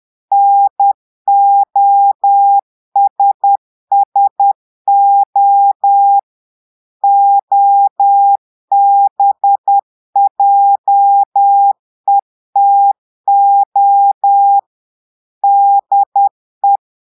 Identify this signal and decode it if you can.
Morse code, heard as a keyed tone, about 10 wpm, 800 Hz. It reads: NOSSO OBJETO DE
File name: Portugais - texte en clair 1 - vitesse - 10 mots minute